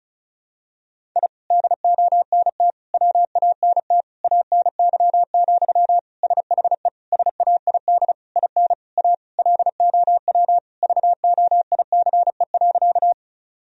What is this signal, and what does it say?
Morse code, heard as a keyed tone, about 36 wpm, 700 Hz. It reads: I DONT WANT ANY, SHE SAID IN A LOW VOICE.
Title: Sentences from Top 1000 Words - Review Part 1 - 35wpm